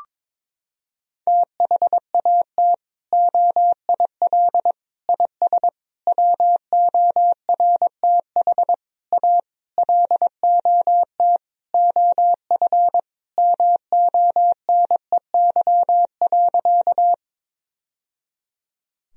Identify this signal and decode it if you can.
Morse code, heard as a keyed tone, about 22 wpm, 700 Hz. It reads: THAT OIL IS WORTH A LOT OF MONEY.
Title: Sentences from Top 1000 Words - Review Part 4 - 22wpm